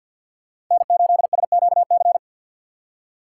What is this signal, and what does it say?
Morse code, heard as a keyed tone, about 51 wpm, 700 Hz. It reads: N8SQC
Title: USA Call Signs - 5 Characters - 2x Character Spacing - 50wpm